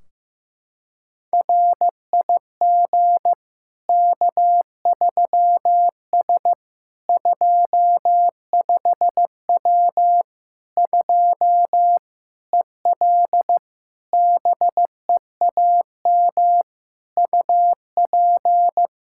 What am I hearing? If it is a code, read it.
Morse code, heard as a keyed tone, about 16 wpm, 700 Hz. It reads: RIG K3S 25W 2 EL BEAM UP
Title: Practice QSO - Exchange 2 - Info - 15wpm